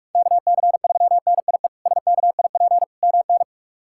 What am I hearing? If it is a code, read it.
Morse code, heard as a keyed tone, about 46 wpm, 700 Hz. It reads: KC3NIE SKIP MN
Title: K1USN SST QSO Practice - 45wpm